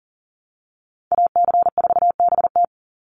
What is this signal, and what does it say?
Morse code, heard as a keyed tone, about 40 wpm, 700 Hz. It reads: AC4BT
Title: Call Signs Repeated 2x - 40wpm